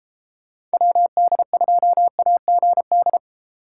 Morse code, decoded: WD2AGD

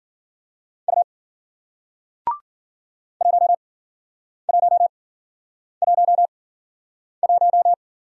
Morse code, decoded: U 1111

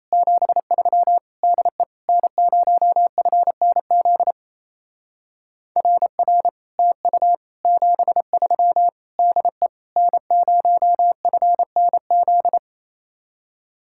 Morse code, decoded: 73 DE N0FNZ RR TU 73 DE N0FNZ

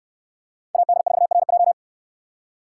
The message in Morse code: AD4RJ

